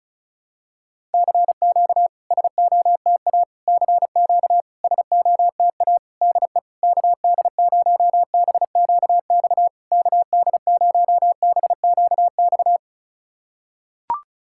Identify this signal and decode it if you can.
Morse code, heard as a keyed tone, about 34 wpm, 690 Hz. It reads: CQ SOTA CQ SOTA DE KD0BQX KD0BQX
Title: SOTA QSO Practice 35wpm